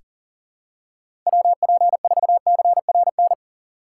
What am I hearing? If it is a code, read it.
Morse code, heard as a keyed tone, about 40 wpm, 700 Hz. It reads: WP4CRN
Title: USA Call Signs - 6 Characters - 6x Character Spacing - 40wpm